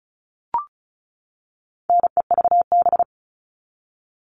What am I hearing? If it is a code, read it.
Morse code, heard as a keyed tone, about 34 wpm, 700 Hz. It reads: NEVB
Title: All Cut Numbers - 4-Digit 35wpm